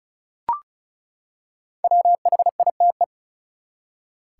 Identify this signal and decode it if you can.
Morse code, heard as a keyed tone, about 34 wpm, 700 Hz. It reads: WHITE
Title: Five Letter Words - 35wpm